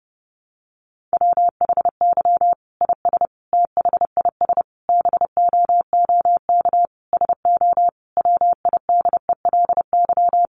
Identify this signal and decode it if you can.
Morse code, heard as a keyed tone, about 30 wpm, 700 Hz. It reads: WHY IS THIS BOOK SO WIDELY
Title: Sentences from Top 1200 Words 30wpm